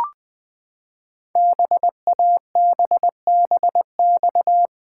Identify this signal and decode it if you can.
Morse code, heard as a keyed tone, about 20 wpm, 700 Hz. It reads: BABBX
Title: Code Groups - 5 Letters - 20wpm